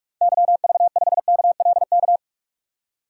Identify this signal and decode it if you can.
Morse code, heard as a keyed tone, about 46 wpm, 700 Hz. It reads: YV5KLK
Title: DXCC-100 Call Signs - 6 Characters - 45wpm